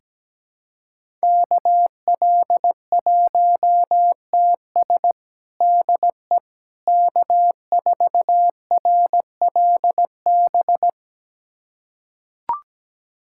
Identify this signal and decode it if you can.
Morse code, heard as a keyed tone, about 17 wpm, 700 Hz. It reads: KL1TS DE K4RLB
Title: USA Callsign DE Callsign 17wpm